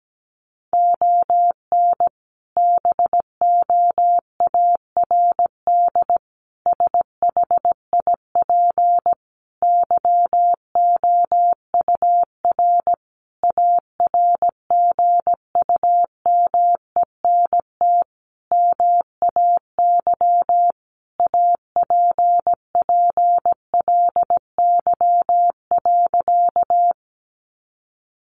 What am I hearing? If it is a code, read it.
Morse code, heard as a keyed tone, about 17 wpm, 700 Hz. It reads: ON BOARD SHIP YOUR ARGUMENT MAY APPLY.
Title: Sentences from Top 1400 Words 17wpm